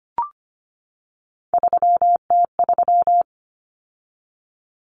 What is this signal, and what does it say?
Morse code, heard as a keyed tone, about 25 wpm, 700 Hz. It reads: 3T3
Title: Common Cut Numbers - 3-Digit 25wpm